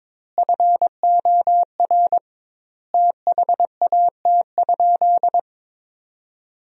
Morse code, heard as a keyed tone, about 22 wpm, 700 Hz. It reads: FOR THAT?
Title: 2x Word-Spacing - Sentences from Top 300 Words - 22wpm